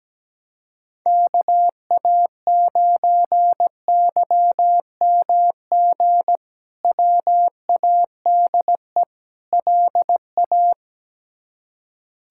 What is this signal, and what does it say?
Morse code, heard as a keyed tone, about 17 wpm, 700 Hz. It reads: KA9YMG WADE LA
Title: K1USN SST QSO Practice - 17wpm